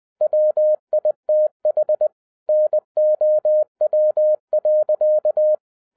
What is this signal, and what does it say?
Morse code, heard as a keyed tone, about 20 wpm, 610 Hz. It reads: WITH NOW.
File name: 2011-07-mp3 / n08 A LOT HAS CHANGED 20wpm